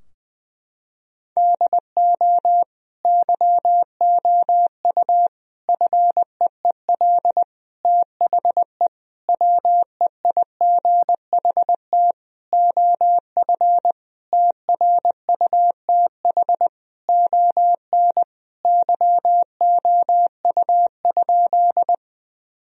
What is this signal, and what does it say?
Morse code, heard as a keyed tone, about 20 wpm, 700 Hz. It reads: DO YOU FEEL THE WEIGHT OF TRUTH ON YOU?
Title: Sentences from Top 1000 Words - Review Part 3 - 20wpm